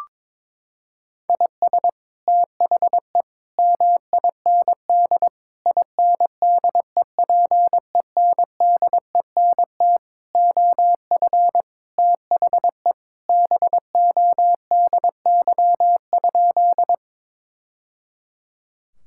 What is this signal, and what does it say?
Morse code, heard as a keyed tone, about 22 wpm, 700 Hz. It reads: IS THE MIND INDEPENDENT OF THE BODY?
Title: Sentences from Top 1000 Words - Review Part 6 - 22wpm